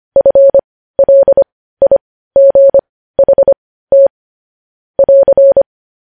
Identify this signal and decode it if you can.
Morse code, heard as a keyed tone, about 25 wpm, spaced as for 15 wpm, 550 Hz. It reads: FLIGHT <AR>
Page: news headlines in morse code at 15 wpm